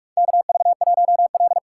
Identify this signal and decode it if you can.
Morse code, heard as a keyed tone, about 44 wpm, 690 Hz. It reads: KV1L
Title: USA Call Signs - 4 Characters - 2x Character Spacing - 45wpm